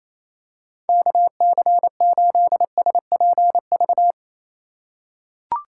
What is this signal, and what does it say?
Morse code, heard as a keyed tone, about 27 wpm, 700 Hz. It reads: KC8SPV
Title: USA Call Signs - 6 Characters - 4x Character Spacing - 28wpm